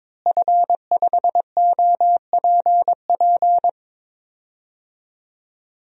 Morse code, heard as a keyed tone, about 22 wpm, 700 Hz. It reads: F5OPP
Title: DXCC-100 Call Signs - 5 Characters - Repeated 2x - 22wpm